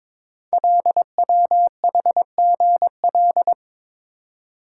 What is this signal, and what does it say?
Morse code, heard as a keyed tone, about 21 wpm, 700 Hz. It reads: LWHGL
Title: Code Groups - 5 Letter:Numbers - 22wpm